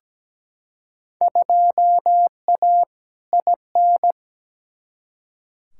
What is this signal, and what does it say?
Morse code, heard as a keyed tone, about 17 wpm, 700 Hz. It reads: 2A IN
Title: Field Day Exchanges - 17wpm